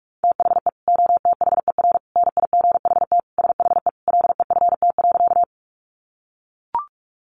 Morse code, heard as a keyed tone, about 44 wpm, 700 Hz. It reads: THE OTHER NIGHT SHE LEFT.
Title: Sentences from Top 300 Words 45wpm